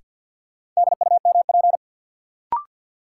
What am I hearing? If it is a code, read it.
Morse code, heard as a keyed tone, about 53 wpm, 700 Hz. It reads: DUMP
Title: Sentences from Top 1300 Words - New Word Drill - 50wpm